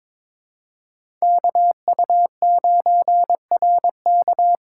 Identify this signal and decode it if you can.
Morse code, heard as a keyed tone, about 22 wpm, 700 Hz. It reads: KU9RK
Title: K1USN SST QSO Practice - 22wpm